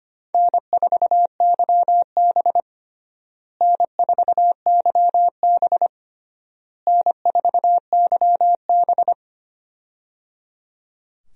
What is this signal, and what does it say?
Morse code, heard as a keyed tone, about 25 wpm, 700 Hz. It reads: N4YB N4YB N4YB
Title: USA Call Signs - 4 Characters - Repeated 3x - 25wpm